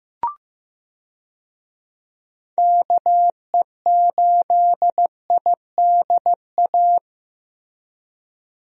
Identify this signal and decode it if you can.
Morse code, heard as a keyed tone, about 15 wpm, 700 Hz. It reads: KE8IDA
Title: USA Call Signs - 6 Characters - Repeated 3x - 15wpm